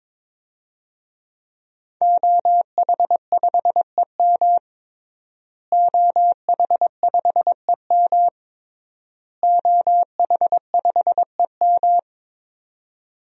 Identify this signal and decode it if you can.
Morse code, heard as a keyed tone, about 22 wpm, 700 Hz. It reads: OH5EM OH5EM OH5EM
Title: DXCC-100 Call Signs - 5 Characters - Repeated 3x - 22wpm